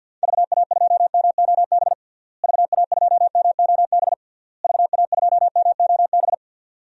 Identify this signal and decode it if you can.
Morse code, heard as a keyed tone, about 47 wpm, 700 Hz. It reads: VA2MOB VA2MOB VA2MOB
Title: DXCC-100 Call Signs - 6 Characters - Repeated 3x - 50wpm